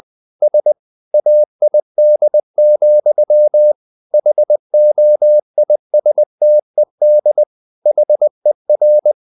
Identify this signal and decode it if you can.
Morse code, heard as a keyed tone, about 20 wpm, 600 Hz. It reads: SAID, HOISTED HER
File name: the old man and the sea / the old man and the sea part 1 20wpm